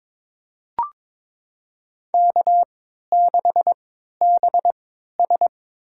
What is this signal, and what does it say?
Morse code, heard as a keyed tone, about 21 wpm, spaced as for 12 wpm, 700 Hz. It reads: K6BS
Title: USA Call Signs - 4 Characters - 3x Character Spacing - 22wpm